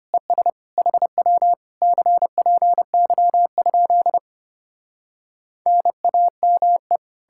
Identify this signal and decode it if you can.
Morse code, heard as a keyed tone, about 33 wpm, 700 Hz. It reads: ES HW CPY? NAME